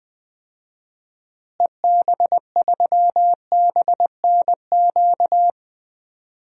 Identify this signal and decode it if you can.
Morse code, heard as a keyed tone, about 20 wpm, 700 Hz. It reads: EB3BNQ
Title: DXCC-100 Call Signs - 6 Characters - Repeated 2x - 20wpm